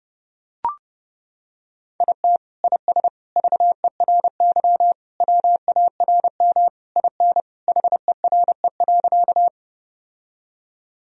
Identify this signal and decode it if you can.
Morse code, heard as a keyed tone, about 29 wpm, 700 Hz. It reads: IT IS VERY WARM IN HERE.
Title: Sentences from Top 1000 Words - Review Part 2 - 30wpm